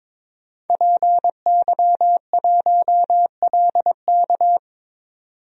Morse code, heard as a keyed tone, about 22 wpm, 700 Hz. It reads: PY1LK
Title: DXCC-100 Call Signs - 5 Characters - 22wpm